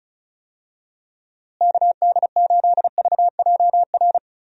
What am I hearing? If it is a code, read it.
Morse code, heard as a keyed tone, about 36 wpm, 700 Hz. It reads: KD8VJR
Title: USA Call Signs - 6 Characters - Repeated 2x - 35wpm